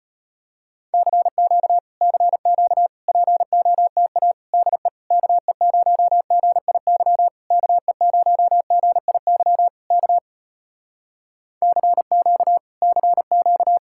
Morse code, heard as a keyed tone, about 33 wpm, 700 Hz. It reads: CQ CQ POTA DE KE0GIY KE0GIY K CQ CQ